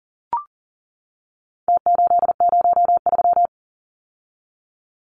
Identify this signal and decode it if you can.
Morse code, heard as a keyed tone, about 40 wpm, 700 Hz. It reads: T803
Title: Numbers - Contest Style - 4-Digit 40wpm